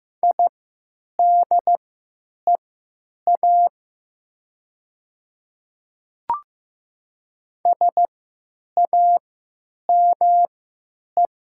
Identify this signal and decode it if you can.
Morse code, heard as a keyed tone, about 15 wpm, spaced as for 8 wpm, 700 Hz. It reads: IDEA SAME
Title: 3x Character Spacing - Top 300 Words - 15wpm